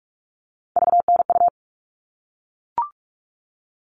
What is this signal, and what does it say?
Morse code, heard as a keyed tone, about 44 wpm, 700 Hz. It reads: VNU
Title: All Cut Numbers - 3-Digit 45wpm